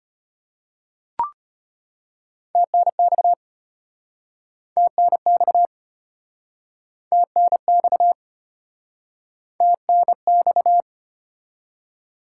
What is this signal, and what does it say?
Morse code, heard as a keyed tone, about 33 wpm, 700 Hz. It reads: TNX TNX TNX TNX